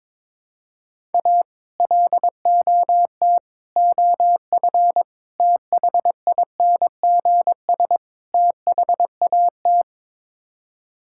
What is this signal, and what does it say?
Morse code, heard as a keyed tone, about 22 wpm, 700 Hz. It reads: A LOT OF THINGS THAT
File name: practice / 22wpm / Build A Phrase - 5-Word Phrases - 22wpm